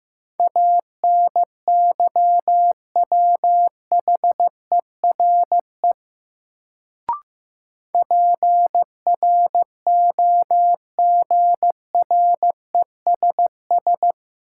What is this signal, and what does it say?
Morse code, heard as a keyed tone, about 15 wpm, 700 Hz. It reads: ANYWHERE PROGRESS
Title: Sentences from Top 1400 Words - New Word Drill - 15wpm